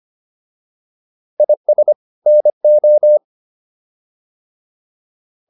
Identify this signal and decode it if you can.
Morse code, heard as a keyed tone, about 25 wpm, 600 Hz. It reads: IS NO